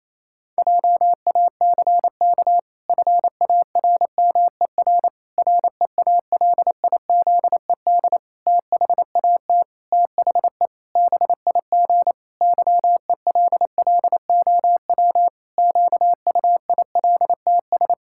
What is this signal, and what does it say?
Morse code, heard as a keyed tone, about 28 wpm, 700 Hz. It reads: JACK FARMER REALIZED THAT THE BIG YELLOW QUILTS
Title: Pangram Challenge v2 - 28wpm